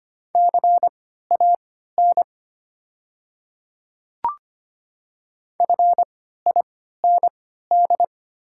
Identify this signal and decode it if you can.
Morse code, heard as a keyed tone, about 25 wpm, spaced as for 14 wpm, 700 Hz. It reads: CAN FIND
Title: 3x Character Spacing - Top 200 Words - 25wpm